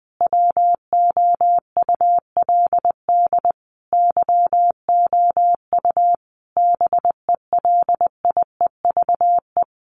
Morse code, heard as a keyed tone, about 20 wpm, 700 Hz. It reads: WOULD YOU BELIEVE
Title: Sentences from Top 1000 Words 20wpm